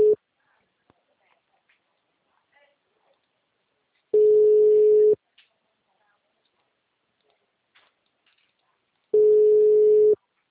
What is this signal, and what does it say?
Ringback tone (Europe, 425 Hz), with speech.